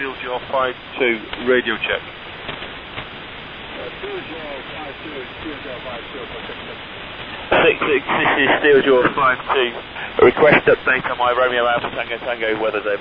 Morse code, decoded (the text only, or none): none